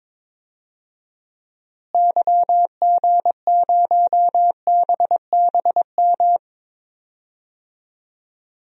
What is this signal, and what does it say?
Morse code, heard as a keyed tone, about 22 wpm, 700 Hz. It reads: YG0BBM
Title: DXCC-100 Call Signs - 6 Characters - Repeated 3x - 22wpm